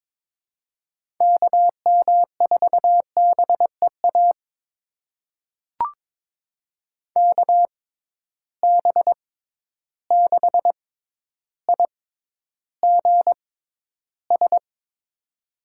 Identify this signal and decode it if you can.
Morse code, heard as a keyed tone, about 22 wpm, spaced as for 11 wpm, 700 Hz. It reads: KM4BEA KB6IGS
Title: USA Call Signs - 6 Characters - 6x Character Spacing - 22wpm